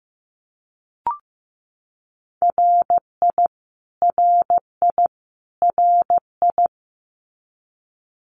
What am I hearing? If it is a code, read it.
Morse code, heard as a keyed tone, about 15 wpm, 700 Hz. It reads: RI RI RI
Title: US States Repeated 3x 15wpm